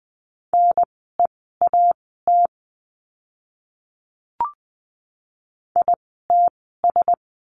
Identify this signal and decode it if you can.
Morse code, heard as a keyed tone, about 20 wpm, spaced as for 15 wpm, 700 Hz. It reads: NEAT ITS
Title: 054 Lesson 10f - Letter r - Words 20wpm